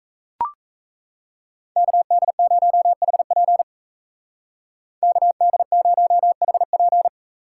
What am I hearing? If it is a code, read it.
Morse code, heard as a keyed tone, about 34 wpm, 700 Hz. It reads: KD0HP KD0HP
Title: Speed Racing - USA Call Signs 28wpm